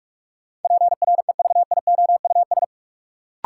Morse code, heard as a keyed tone, about 46 wpm, 700 Hz. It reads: PREVIOUS
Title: Sentences from Top 1500 Words - New Word Drill - 45wpm